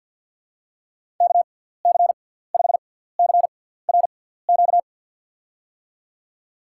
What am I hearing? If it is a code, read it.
Morse code, heard as a keyed tone, about 47 wpm, spaced as for 17 wpm, 700 Hz. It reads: KC5CRQ
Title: USA Call Signs - 6 Characters - 6x Character Spacing - 50wpm